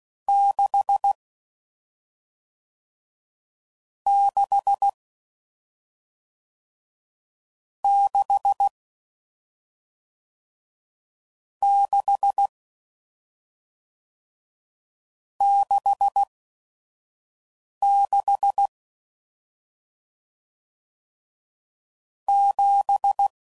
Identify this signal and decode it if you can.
Morse code, heard as a keyed tone, about 16 wpm, 800 Hz. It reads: 6 6 6 6 6 6 7